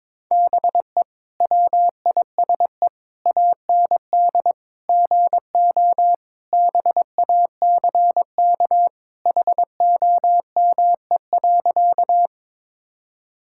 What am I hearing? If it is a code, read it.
Morse code, heard as a keyed tone, about 22 wpm, 700 Hz. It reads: BE WISE AND GO BACK HOME.
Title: Sentences from Top 1600 Words 22wpm